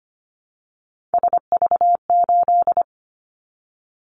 Morse code, heard as a keyed tone, about 25 wpm, 700 Hz. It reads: SV8
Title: Callsign Prefixes for Top 50 DXCC - 25wpm